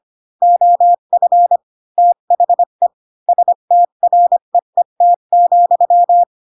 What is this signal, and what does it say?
Morse code, heard as a keyed tone, about 25 wpm, 700 Hz. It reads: OF THE STREET,
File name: Pinocchio / Pinocchio ch3 20 25 ik1hge